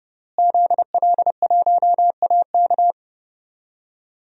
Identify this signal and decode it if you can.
Morse code, heard as a keyed tone, about 32 wpm, 700 Hz. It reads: ZL1AK
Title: DXCC-100 Call Signs - 5 Characters - Repeated 3x - 30wpm